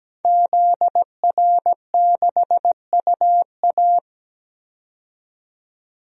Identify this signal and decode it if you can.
Morse code, heard as a keyed tone, about 17 wpm, 700 Hz. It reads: ZR6UA
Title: DXCC-100 Call Signs - 5 Characters - 17wpm